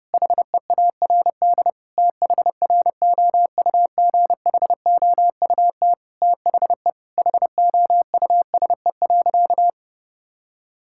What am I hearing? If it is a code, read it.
Morse code, heard as a keyed tone, about 32 wpm, 700 Hz. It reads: HEARD THROUGHOUT THE HOUSE.
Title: Sentences from Top 1000 Words - Review Part 5 - 30wpm